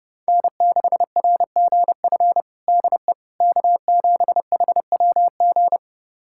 Morse code, heard as a keyed tone, about 32 wpm, 700 Hz. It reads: N6RGF DE K7HWG
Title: USA Callsign DE Callsign - 5 Characters - 30wpm